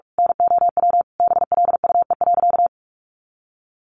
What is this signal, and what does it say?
Morse code, heard as a keyed tone, about 44 wpm, 700 Hz. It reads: NOW BLUE.
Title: A Sound Of Thunder 45wpm